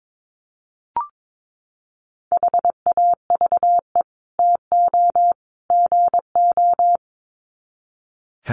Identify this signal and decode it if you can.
Morse code, heard as a keyed tone, about 22 wpm, 700 Hz. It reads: HAVE TO GO
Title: Sets of 3 Words 22wpm